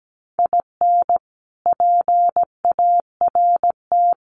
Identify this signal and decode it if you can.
Morse code, heard as a keyed tone, about 17 wpm, 700 Hz. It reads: IN PART
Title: Sets of 5 Words Encore2 17wpm